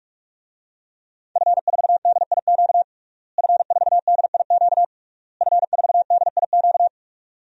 Morse code, heard as a keyed tone, about 46 wpm, 700 Hz. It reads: F4DIQ F4DIQ F4DIQ
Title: DXCC-100 Call Signs - 5 Characters - Repeated 3x - 45wpm